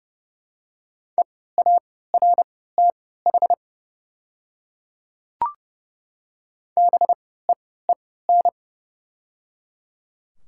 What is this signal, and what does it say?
Morse code, heard as a keyed tone, about 30 wpm, spaced as for 17 wpm, 700 Hz. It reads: EARTH BEEN
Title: 3x Character Spacing - Top 300 Words - 30wpm